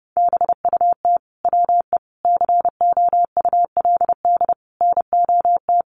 Morse code, heard as a keyed tone, about 30 wpm, 700 Hz. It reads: BUT WE COULD NOT
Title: Sentences from Top 1100 Words 30wpm